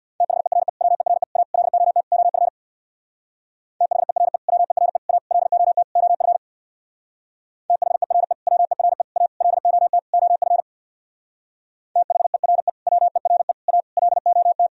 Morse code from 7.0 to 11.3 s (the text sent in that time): THERE WERE A LOT OF